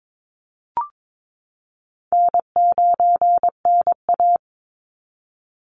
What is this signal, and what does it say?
Morse code, heard as a keyed tone, about 22 wpm, 690 Hz. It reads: N9NA
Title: Easy Call Signs Repeated 2x - 22wpm